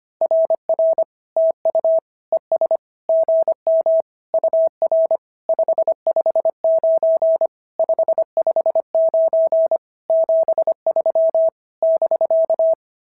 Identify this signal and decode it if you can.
Morse code, heard as a keyed tone, about 25 wpm, 650 Hz. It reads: RR TU ES GM UR 559 559 73 <BK>
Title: SOTA QSO Practice 25wpm